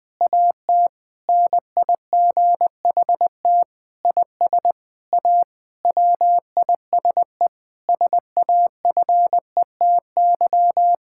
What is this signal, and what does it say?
Morse code, heard as a keyed tone, about 21 wpm, 700 Hz. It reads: AT NIGHT IS A WISE SAFETY
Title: Difficult Sentences with Hints - 20wpm